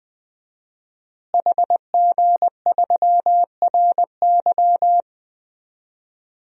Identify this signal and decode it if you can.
Morse code, heard as a keyed tone, about 21 wpm, 700 Hz. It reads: HG3RY
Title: DXCC-100 Call Signs - 5 Characters - Repeated 3x - 20wpm